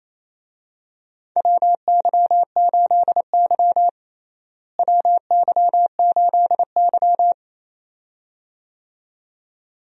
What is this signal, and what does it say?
Morse code, heard as a keyed tone, about 28 wpm, 700 Hz. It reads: WY8Y WY8Y
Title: USA Call Signs - 4 Characters - Repeated 2x - 28wpm